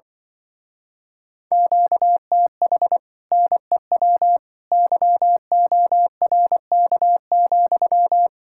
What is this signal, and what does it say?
Morse code, heard as a keyed tone, about 24 wpm, 700 Hz. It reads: QTH NEW YORK,
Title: Speed-Racing Common QSO Phrases - 20wpm